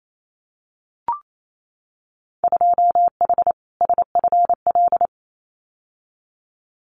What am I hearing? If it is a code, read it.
Morse code, heard as a keyed tone, about 28 wpm, 700 Hz. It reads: 2H SFL
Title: Winter Field Day Exchanges Encore 28wpm